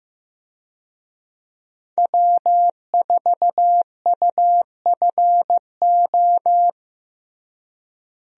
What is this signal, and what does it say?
Morse code, heard as a keyed tone, about 15 wpm, 700 Hz. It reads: W4UFO
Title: USA Call Signs - 5 Characters - Repeated 2x - 15wpm